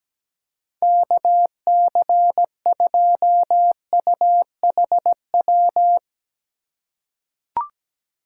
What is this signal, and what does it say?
Morse code, heard as a keyed tone, about 17 wpm, 700 Hz. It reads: KC2UHW
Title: USA Call Signs - 2x Character Spacing - 17wpm